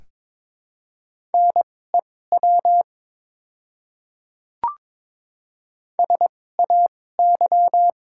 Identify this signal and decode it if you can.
Morse code, heard as a keyed tone, about 22 wpm, spaced as for 16 wpm, 700 Hz. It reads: NEW SAY
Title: 2x Character Spacing - Top 200 Words - 22wpm